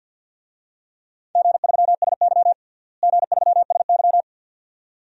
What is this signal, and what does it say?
Morse code, heard as a keyed tone, about 53 wpm, 700 Hz. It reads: G3SY G3SY
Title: DXCC-100 Call Signs - 4 Characters - Repeated 2x - 50wpm